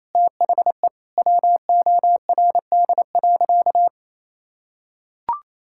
Morse code, heard as a keyed tone, about 28 wpm, 700 Hz. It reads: THE WORD.
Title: Sentences from Top 1000 Words - Review Part 6 - 28wpm